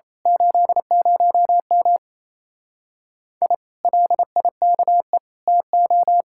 Morse code, heard as a keyed tone, about 27 wpm, 700 Hz. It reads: 80M I LIKE TO
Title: Speed-Racing Common QSO Phrases Encore - 28wpm